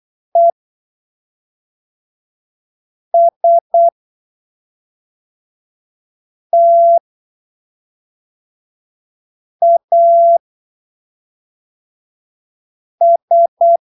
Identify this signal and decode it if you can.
Morse code, heard as a keyed tone, about 8 wpm, spaced as for 3 wpm, 680 Hz. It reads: ESTAS
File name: arquivos / texto / texto 26 08ppm